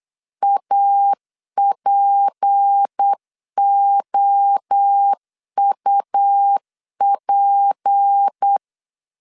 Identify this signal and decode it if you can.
Morse code, heard as a keyed tone, about 8 wpm, 800 Hz. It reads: APOUP
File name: LEZIONI / 6 PKX0 / 6 P 8wpm 3025